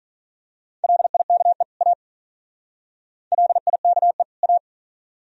Morse code, heard as a keyed tone, about 41 wpm, 700 Hz. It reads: LIKE A LIKE A
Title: Speed-Racing - Sets of 2 Words 35wpm